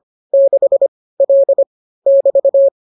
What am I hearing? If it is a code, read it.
Morse code, heard as a keyed tone, about 25 wpm, 550 Hz. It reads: 6 L <BT>